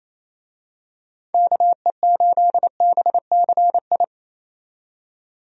Morse code, heard as a keyed tone, about 28 wpm, 700 Hz. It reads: KE8BCI
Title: USA Call Signs - 6 Characters - 28wpm